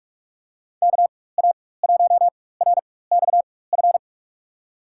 Morse code, heard as a keyed tone, about 46 wpm, spaced as for 21 wpm, 700 Hz. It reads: KA1RXF
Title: USA Call Signs - 6 Characters - 4x Character Spacing - 45wpm